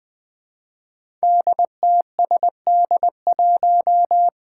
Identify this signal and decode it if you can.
Morse code, heard as a keyed tone, about 20 wpm, 700 Hz. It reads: DTSD1
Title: Five Challenging Letter-Number Combinations 20wpm